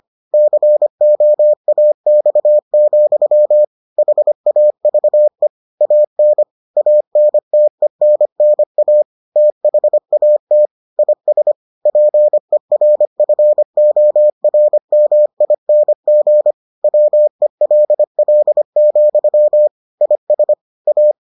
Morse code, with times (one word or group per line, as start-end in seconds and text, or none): 0.3-3.6 COAX,
4.0-5.5 HAVE
5.8-6.4 AN
6.8-9.0 ANTENNA
9.4-10.7 THAT
11.0-11.5 IS
11.9-16.5 PERFORMING
16.8-19.7 WELL,
20.0-20.5 IS
20.9-21.1 A